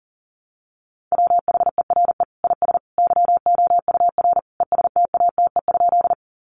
Morse code, heard as a keyed tone, about 40 wpm, 700 Hz. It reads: WHERE IS YOUR ESTATE?
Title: Sentences from Top 1200 Words 40wpm